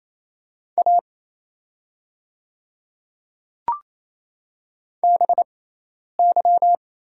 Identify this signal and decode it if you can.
Morse code, heard as a keyed tone, about 27 wpm, spaced as for 10 wpm, 700 Hz. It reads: A BY
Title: 6x Character Spacing - Top 100 Words - 28wpm